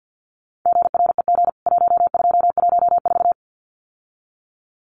Morse code, heard as a keyed tone, about 50 wpm, 700 Hz. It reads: GREG 1214
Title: CWT Exchange 50wpm